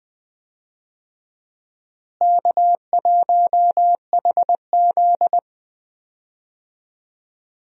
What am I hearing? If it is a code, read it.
Morse code, heard as a keyed tone, about 20 wpm, 700 Hz. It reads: K1HZ
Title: USA Call Signs - 4 Characters - Repeated 2x - 20wpm